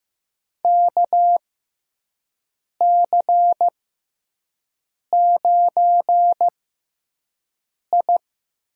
Morse code, heard as a keyed tone, about 15 wpm, spaced as for 5 wpm, 700 Hz. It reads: KC9I
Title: USA Call Signs - 6 Characters - 6x Character Spacing - 15wpm